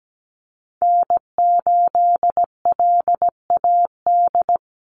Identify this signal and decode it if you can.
Morse code, heard as a keyed tone, about 17 wpm, 700 Hz. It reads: N8LAD
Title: USA Call Signs 17wpm